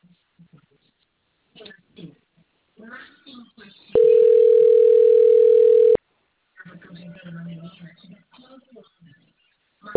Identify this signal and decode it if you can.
Ringback tone (North America), 440 + 480 Hz, with speech.
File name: rg-1600-unknown- - - .41640